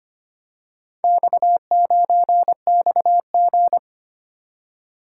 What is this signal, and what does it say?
Morse code, heard as a keyed tone, about 25 wpm, 700 Hz. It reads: X9XG
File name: practice / 25wpm / Four Letter-Number Combinations - 25wpm